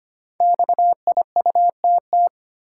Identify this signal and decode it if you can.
Morse code, heard as a keyed tone, about 25 wpm, 700 Hz. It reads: XIUTT
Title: Code Groups - 5 Letters - 25wpm